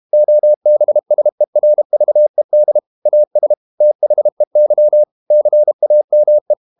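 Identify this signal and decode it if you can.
Morse code, heard as a keyed tone, about 32 wpm, 600 Hz. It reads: OBSERVED AS THEY CAME
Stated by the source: Morse code practice at 32 wpm from CW med Gnister, reading Ken Follett's novel Never in English.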